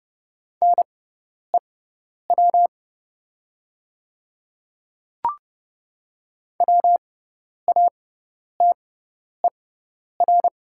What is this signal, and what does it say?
Morse code, heard as a keyed tone, about 29 wpm, spaced as for 10 wpm, 690 Hz. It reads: NEW WATER
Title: 6x Character Spacing - Top 200 Words - 30wpm